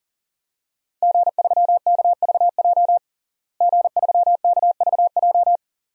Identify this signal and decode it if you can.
Morse code, heard as a keyed tone, about 40 wpm, 700 Hz. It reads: G3KVJ G3KVJ
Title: DXCC-100 Call Signs - 5 Characters - Repeated 2x - 40wpm